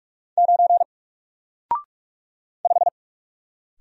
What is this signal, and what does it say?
Morse code, heard as a keyed tone, about 44 wpm, 700 Hz. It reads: 9 5
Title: Single Character (Letter:Number:Punctuation:Prosign) Alt 45wpm